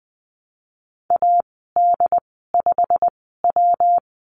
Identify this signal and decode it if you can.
Morse code, heard as a keyed tone, about 20 wpm, spaced as for 15 wpm, 700 Hz. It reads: AD5W
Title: 167 Lesson 27g - Number 7 - Call Signs 20wpm